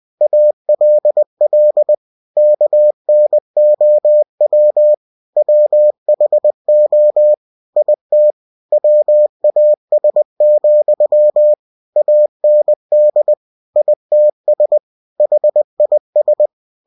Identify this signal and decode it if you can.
Morse code, heard as a keyed tone, about 20 wpm, 600 Hz. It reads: ALL KNOW WHO IT WAS, AND ITS HIS
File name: cw-books / winnie-the-pooh / winnie-20wpm-page-0107